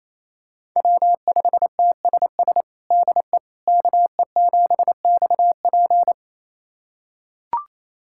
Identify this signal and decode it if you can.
Morse code, heard as a keyed tone, about 28 wpm, 700 Hz. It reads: W5TSS DE KE7XP
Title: USA Callsign DE Callsign 28wpm